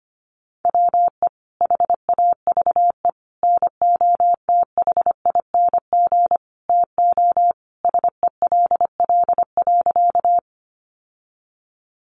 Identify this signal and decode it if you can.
Morse code, heard as a keyed tone, about 25 wpm, 700 Hz. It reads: WE HAVE NOTHING TO SELL.
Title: Sentences from Top 1100 Words 25wpm